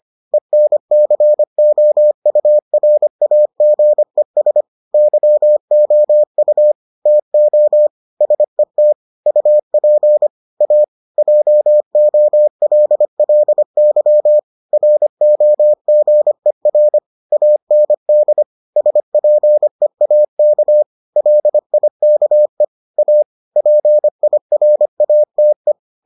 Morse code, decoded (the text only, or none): ENCOURAGES YOU TO SET UP A JOLLY ROGER AND SPEAK LIKE A PIRATE